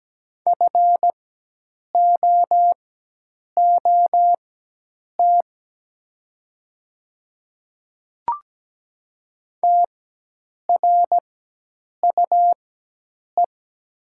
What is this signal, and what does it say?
Morse code, heard as a keyed tone, about 17 wpm, spaced as for 8 wpm, 700 Hz. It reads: FOOT TRUE
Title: 4x Character Spacing - Top 500 Words - 17wpm